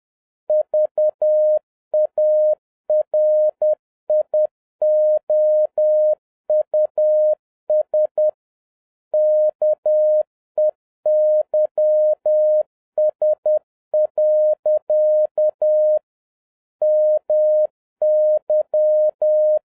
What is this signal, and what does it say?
Morse code, heard as a keyed tone, about 10 wpm, 610 Hz. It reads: VARIOUS KEYS. MY